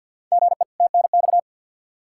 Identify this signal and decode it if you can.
Morse code, heard as a keyed tone, about 47 wpm, 700 Hz. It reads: GE TNX
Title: Practice QSO - Exchange 1 - Beginning - 50wpm